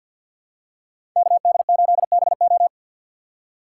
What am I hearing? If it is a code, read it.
Morse code, heard as a keyed tone, about 47 wpm, 700 Hz. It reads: KD8BO
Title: USA Call Signs - 5 Characters - 4x Character Spacing - 50wpm